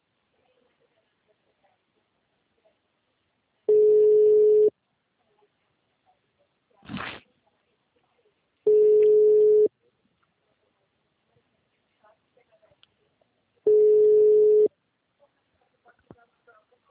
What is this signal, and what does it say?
Ringback tone (Europe, 425 Hz).